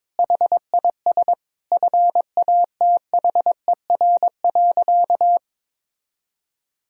Morse code, heard as a keyed tone, about 22 wpm, 700 Hz. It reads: HIS FATHER.
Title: Sentences from Top 1000 Words - Review Part 2 - 22wpm